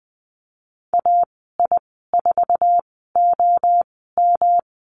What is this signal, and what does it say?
Morse code, heard as a keyed tone, about 20 wpm, spaced as for 15 wpm, 700 Hz. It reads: AI4OM
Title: 207 Lesson 33g - Letter j - Call Signs 20wpm